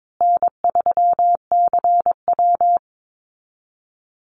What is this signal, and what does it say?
Morse code, heard as a keyed tone, about 22 wpm, 700 Hz. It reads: N3CW
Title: Easy Call Signs Repeated 3x - 22wpm